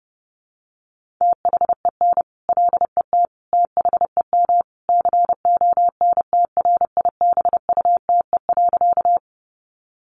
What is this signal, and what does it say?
Morse code, heard as a keyed tone, about 30 wpm, 700 Hz. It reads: THEN LET THEM CONTRIBUTE.
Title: Sentences from Top 1400 Words 30wpm